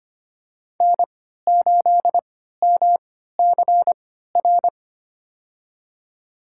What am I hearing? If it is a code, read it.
Morse code, heard as a keyed tone, about 25 wpm, spaced as for 14 wpm, 700 Hz. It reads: N8MCR
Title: USA Call Signs - 5 Characters - 3x Character Spacing - 25wpm